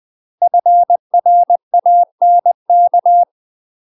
Morse code, heard as a keyed tone, about 20 wpm, 700 Hz. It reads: FRANK
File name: UK MASTER QSOs TRAINING / 20WPM / UK MASTER QSOs 0055